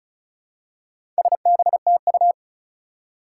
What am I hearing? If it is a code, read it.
Morse code, heard as a keyed tone, about 36 wpm, 700 Hz. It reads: SBTU
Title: Four Challenging Letter-Number Combinations 35wpm